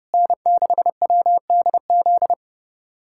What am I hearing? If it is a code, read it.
Morse code, heard as a keyed tone, about 29 wpm, 700 Hz. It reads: N6WDZ
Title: USA Callsign DE Callsign - 5 Characters - 30wpm